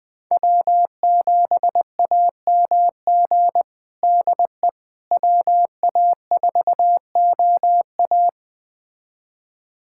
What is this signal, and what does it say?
Morse code, heard as a keyed tone, about 20 wpm, 700 Hz. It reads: W7AMG DE WA4OA
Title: USA Callsign DE Callsign - 5 Characters - 20wpm